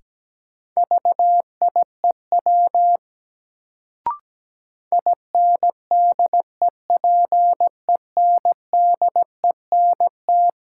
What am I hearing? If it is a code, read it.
Morse code, heard as a keyed tone, about 17 wpm, 700 Hz. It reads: VIEW INDEPENDENT
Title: Sentences from Top 900 Words - New Word Drill - 17wpm